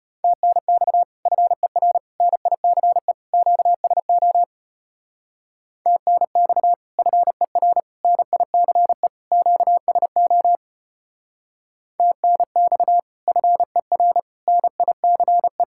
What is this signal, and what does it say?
Morse code, heard as a keyed tone, about 33 wpm, 700 Hz. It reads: TNX FER NICE QSO TNX FER NICE QSO TNX FER NICE